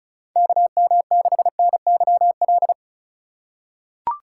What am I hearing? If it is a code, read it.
Morse code, heard as a keyed tone, about 36 wpm, 700 Hz. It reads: KM6NYL
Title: USA Call Signs - 6 Characters - 4x Character Spacing - 35wpm